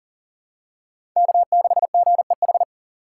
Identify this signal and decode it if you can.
Morse code, heard as a keyed tone, about 40 wpm, 700 Hz. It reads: K6GEH
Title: USA Call Signs - 2x Character Spacing - 40wpm